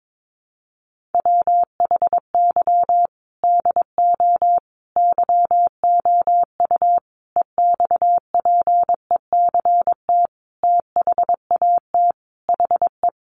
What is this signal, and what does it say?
Morse code, heard as a keyed tone, about 22 wpm, 700 Hz. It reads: WHY DO YOU EXPECT THAT HE
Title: Sentences from Top 1100 Words 22wpm